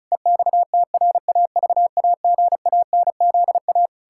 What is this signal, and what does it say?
Morse code, heard as a keyed tone, about 36 wpm, 700 Hz. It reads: EXTRAVAGANZA
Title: Pangram Challenge v2 - 35wpm